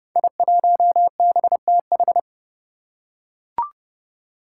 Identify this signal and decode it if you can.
Morse code, heard as a keyed tone, about 31 wpm, 700 Hz. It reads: I1BTH
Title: Five Challenging Letter-Number Combinations 30wpm